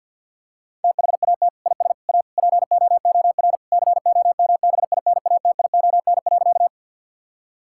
Morse code, heard as a keyed tone, about 52 wpm, 700 Hz. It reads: THAT IS A POOR COMBINATION.
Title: Sentences from Top 1000 Words - Review Part 6 - 50wpm